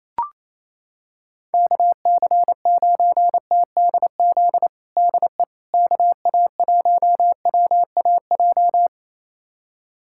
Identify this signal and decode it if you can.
Morse code, heard as a keyed tone, about 28 wpm, 700 Hz. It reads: KC9TDZ DE KA1WAJ
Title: USA Callsign DE Callsign - 6 Characters - 28wpm